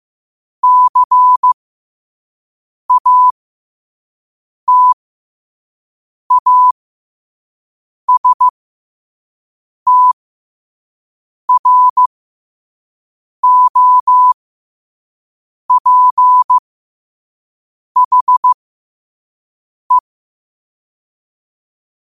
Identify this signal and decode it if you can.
Morse code, heard as a keyed tone, about 15 wpm, spaced as for 5 wpm, 1000 Hz. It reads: CATASTROPHE